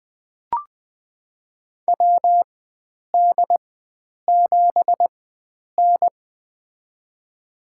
Morse code, heard as a keyed tone, about 20 wpm, spaced as for 9 wpm, 700 Hz. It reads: WD7N
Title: USA Call Signs - 4 Characters - 4x Character Spacing - 20wpm